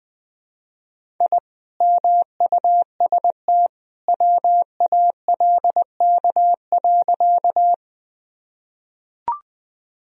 Morse code, heard as a keyed tone, about 19 wpm, 700 Hz. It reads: I MUST WALK.
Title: Sentences from Top 1000 Words - Review Part 1 - 20wpm